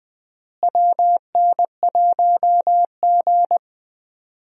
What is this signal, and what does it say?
Morse code, heard as a keyed tone, about 20 wpm, 700 Hz. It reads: WN1G
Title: USA Call Signs - 4 Characters - Repeated 3x - 20wpm